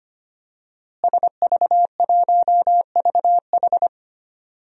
Morse code, heard as a keyed tone, about 25 wpm, 700 Hz. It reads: SV1VH
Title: DXCC-100 Call Signs - 5 Characters - Repeated 2x - 25wpm